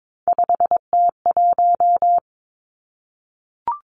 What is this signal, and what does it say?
Morse code, heard as a keyed tone, about 22 wpm, 700 Hz. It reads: 5T1
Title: Three Challenging Letter-Number Combinations 22wpm